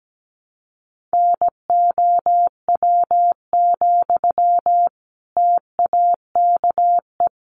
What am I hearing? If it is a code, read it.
Morse code, heard as a keyed tone, about 17 wpm, 700 Hz. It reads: NOW, TAKE
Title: Sentences from Top 400 Words 17wpm